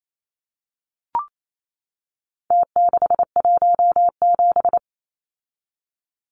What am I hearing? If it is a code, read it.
Morse code, heard as a keyed tone, about 28 wpm, 700 Hz. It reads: T617
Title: Numbers - Contest Style - 4-Digit 28wpm